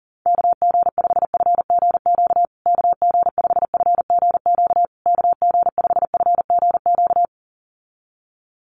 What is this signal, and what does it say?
Morse code, heard as a keyed tone, about 40 wpm, 700 Hz. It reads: KG5FGQ KG5FGQ KG5FGQ
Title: USA Call Signs Repeated 3x 40wpm